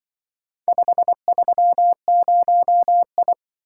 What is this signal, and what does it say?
Morse code, heard as a keyed tone, about 24 wpm, 700 Hz. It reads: 530I
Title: Speed-Racing - Four Letter-Number Combinations - 20wpm